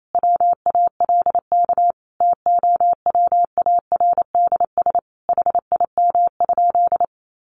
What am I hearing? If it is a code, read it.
Morse code, heard as a keyed tone, about 28 wpm, 700 Hz. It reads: WALK TOWARDS HIM?
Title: Sentences from Top 1100 Words 28wpm